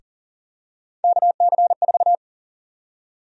Morse code, heard as a keyed tone, about 39 wpm, 700 Hz. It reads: KC4
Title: USA Call Sign Prefixes - Repeated 2x - 40wpm